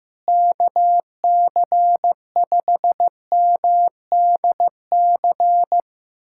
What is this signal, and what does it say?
Morse code, heard as a keyed tone, about 15 wpm, 700 Hz. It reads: KC5MDC
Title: USA Callsign DE Callsign - 6 Characters - 15wpm